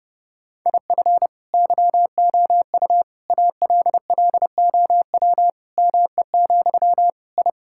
Morse code, heard as a keyed tone, about 32 wpm, 700 Hz. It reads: IF YOU ALLOW ME, I
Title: Sentences from Top 1000 Words - Review Part 5 - 30wpm